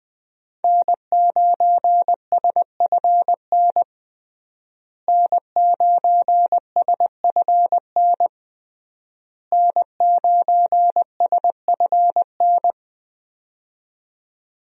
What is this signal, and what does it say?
Morse code, heard as a keyed tone, about 20 wpm, 700 Hz. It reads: N9SFN N9SFN N9SFN
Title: USA Call Signs - 5 Characters - Repeated 3x - 20wpm